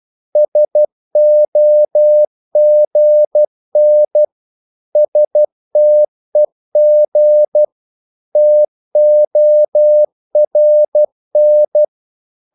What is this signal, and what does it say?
Morse code, heard as a keyed tone, about 12 wpm, 600 Hz. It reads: SOGN STEG TORN
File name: Korte ord DK 12wpm